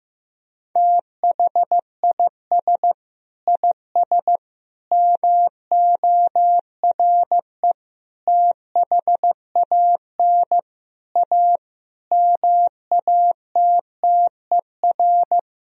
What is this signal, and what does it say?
Morse code, heard as a keyed tone, about 15 wpm, 700 Hz. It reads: THIS IS MORE THAN A MATTER
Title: Sentences from Top 1000 Words - Review Part 3 - 15wpm